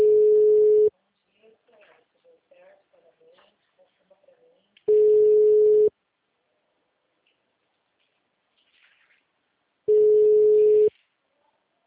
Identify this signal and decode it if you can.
Ringback tone (Europe, 425 Hz), with speech.